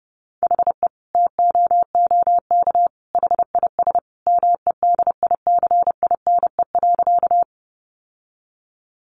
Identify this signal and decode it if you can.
Morse code, heard as a keyed tone, about 30 wpm, 700 Hz. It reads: HE TOOK HIS MEDICINE.
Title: Sentences from Top 900 Words 30wpm